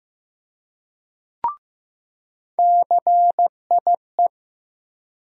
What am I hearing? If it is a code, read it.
Morse code, heard as a keyed tone, about 15 wpm, 700 Hz. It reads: CIE
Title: USA Call Sign Postfixes - 3 Letters 15wpm